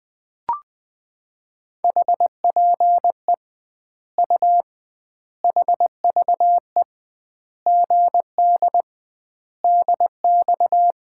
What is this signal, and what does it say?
Morse code, heard as a keyed tone, about 21 wpm, 700 Hz. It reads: HPE U HVE GD DX
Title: Common QSO Phrases - 2x Word-Spacing - 20wpm